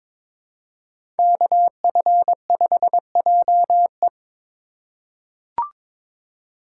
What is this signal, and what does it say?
Morse code, heard as a keyed tone, about 22 wpm, 700 Hz. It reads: KF5JE
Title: USA Call Signs - 5 Characters - 2x Character Spacing - 22wpm